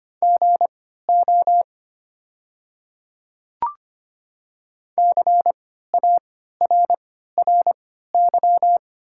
Morse code, heard as a keyed tone, about 25 wpm, spaced as for 14 wpm, 690 Hz. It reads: GO CARRY
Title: 3x Character Spacing - Top 500 Words - 25wpm